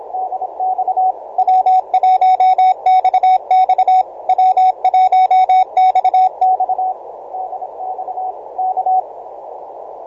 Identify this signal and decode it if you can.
Morse code, heard as a keyed tone, about 27 wpm, 750 Hz, with radio noise. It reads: DX W1XX W1XX DX K
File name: simple-file-list / W1XX 6m cw